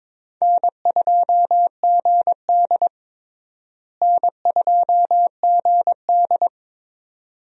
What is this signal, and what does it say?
Morse code, heard as a keyed tone, about 22 wpm, 700 Hz. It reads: N2GD N2GD
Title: USA Call Signs - 4 Characters - Repeated 2x - 22wpm